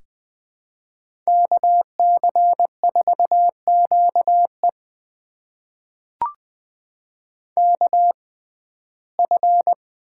Morse code, heard as a keyed tone, about 21 wpm, spaced as for 15 wpm, 700 Hz. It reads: KC4QE KF
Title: USA Call Signs - 6x Character Spacing - 20wpm